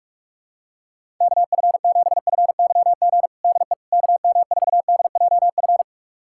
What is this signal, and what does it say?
Morse code, heard as a keyed tone, about 45 wpm, 700 Hz. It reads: KF7FYG DE KM4DJF
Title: USA Callsign DE Callsign 45wpm